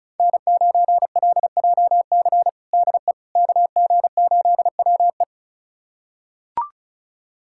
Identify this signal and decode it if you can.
Morse code, heard as a keyed tone, about 36 wpm, 700 Hz. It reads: N9LJC DE KG8WE
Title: USA Callsign DE Callsign - 5 Characters - 35wpm